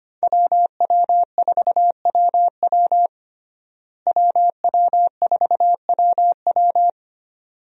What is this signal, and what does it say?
Morse code, heard as a keyed tone, about 25 wpm, 700 Hz. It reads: WW4WW WW4WW
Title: USA Call Signs - 5 Characters - Repeated 2x - 25wpm